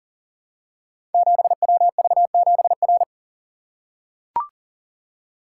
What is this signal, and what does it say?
Morse code, heard as a keyed tone, about 39 wpm, 700 Hz. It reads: 7WV7R
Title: Code Groups - 5 Letter:Numbers - 40wpm